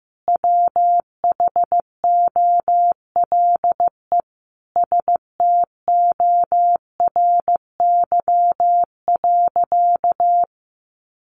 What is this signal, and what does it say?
Morse code, heard as a keyed tone, about 15 wpm, 700 Hz. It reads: WHOLE STORY.
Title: Sentences from Top 1100 Words 15wpm